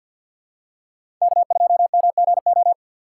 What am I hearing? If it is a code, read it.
Morse code, heard as a keyed tone, about 51 wpm, 700 Hz. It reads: K2MGO
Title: USA Call Signs - 5 Characters - 2x Character Spacing - 50wpm